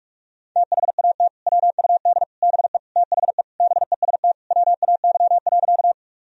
Morse code, heard as a keyed tone, about 43 wpm, 700 Hz. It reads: THAT WUD BE THE BEST WAY.
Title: Sentences from Top 400 Words and Abbreviations 45wpm